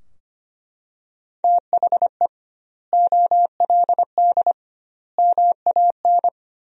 Morse code, heard as a keyed tone, about 25 wpm, 700 Hz. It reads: THE OLD MAN
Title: 2x Word-Spacing - Sentences from Top 500 Words - 25wpm